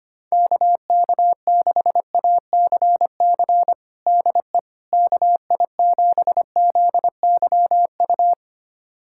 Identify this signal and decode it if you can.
Morse code, heard as a keyed tone, about 25 wpm, 700 Hz. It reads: KK6ACC DE KI7ZYU
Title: USA Callsign DE Callsign - 6 Characters - 25wpm